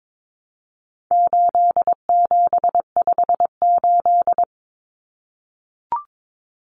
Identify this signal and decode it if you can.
Morse code, heard as a keyed tone, about 22 wpm, 700 Hz. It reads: 8758
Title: Numbers - 4-Digit 22wpm